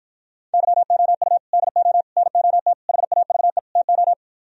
Morse code, heard as a keyed tone, about 55 wpm, 700 Hz. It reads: YOU DO NOT HAVE TO